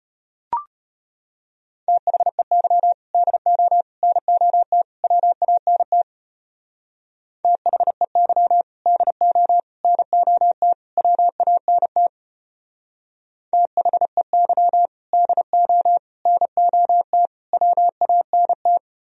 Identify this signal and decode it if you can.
Morse code, heard as a keyed tone, about 31 wpm, 700 Hz. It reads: THEY DO NOT WANT THEY DO NOT WANT THEY DO NOT WANT
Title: Speed-Racing - Sets of 4 Words 25wpm